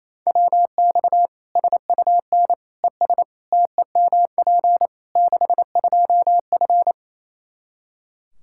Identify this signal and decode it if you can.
Morse code, heard as a keyed tone, about 28 wpm, 700 Hz. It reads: WX SUN ES TEMP 62F
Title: Practice QSO - Exchange 2 - Info - 28wpm